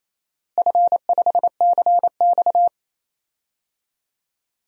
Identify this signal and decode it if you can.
Morse code, heard as a keyed tone, about 28 wpm, 700 Hz. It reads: F5CX
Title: DXCC-100 Call Signs - 4 Characters - Repeated 2x - 28wpm